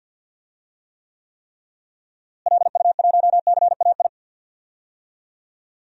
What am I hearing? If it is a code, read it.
Morse code, heard as a keyed tone, about 50 wpm, 700 Hz. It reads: LU1CAI
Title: DXCC-100 Call Signs - 6 Characters - Repeated 3x - 50wpm